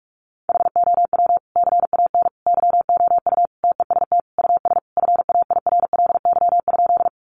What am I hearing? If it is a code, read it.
Morse code, heard as a keyed tone, about 44 wpm, 700 Hz. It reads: HOW CAN YOU TEST US FAIRLY?
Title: Sentences from Top 1100 Words 45wpm